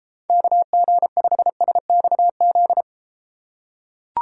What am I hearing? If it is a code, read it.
Morse code, heard as a keyed tone, about 33 wpm, 700 Hz. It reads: KG5SXZ